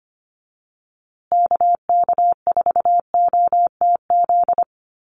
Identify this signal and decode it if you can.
Morse code, heard as a keyed tone, about 25 wpm, 700 Hz. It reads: KK4OTZ
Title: USA Call Signs Repeated 3x 25wpm